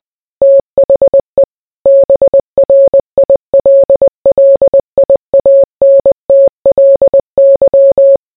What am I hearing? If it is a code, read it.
Morse code, heard as a keyed tone, about 20 wpm, 550 Hz. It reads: THE BRILLIANTLY